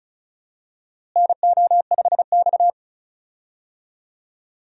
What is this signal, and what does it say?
Morse code, heard as a keyed tone, about 34 wpm, 700 Hz. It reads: NO5X
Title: USA Call Signs - 4 Characters - Repeated 2x - 35wpm